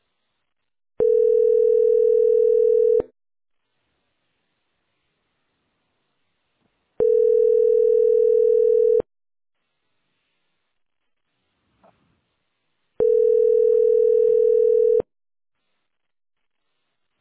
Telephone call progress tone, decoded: ringback tone

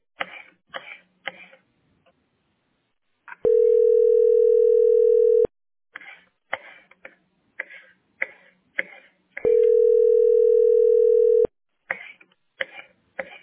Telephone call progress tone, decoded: ringback tone

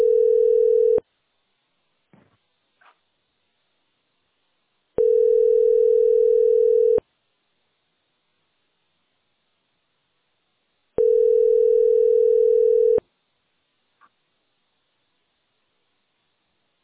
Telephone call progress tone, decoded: ringback tone

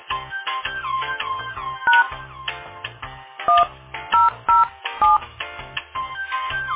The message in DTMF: # 1007

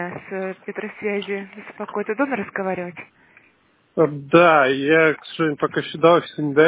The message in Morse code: none

